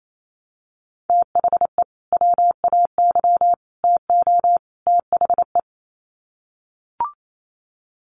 Morse code, heard as a keyed tone, about 28 wpm, 700 Hz. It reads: THE WAY TO THE
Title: Sets of 4 Words 28wpm